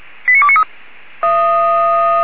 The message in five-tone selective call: 89293

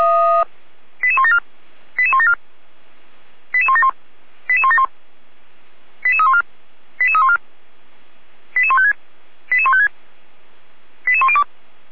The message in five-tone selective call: 88174 88174 88171 88171 88314 88314 88156 88156 88102

